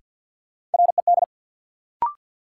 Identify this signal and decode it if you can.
Morse code, heard as a keyed tone, about 48 wpm, 700 Hz. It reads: RED
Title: Sentences from Top 700 Words - New Word Drill - 50wpm